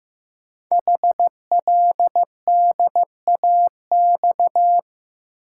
Morse code, heard as a keyed tone, about 15 wpm, 700 Hz. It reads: HLDAX
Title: Code Groups - 5 Letters - 15wpm